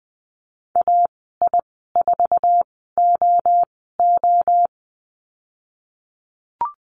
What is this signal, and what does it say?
Morse code, heard as a keyed tone, about 20 wpm, spaced as for 14 wpm, 700 Hz. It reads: AI4OO
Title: 048 Lesson 9g - Number 4 - Call Signs 20wpm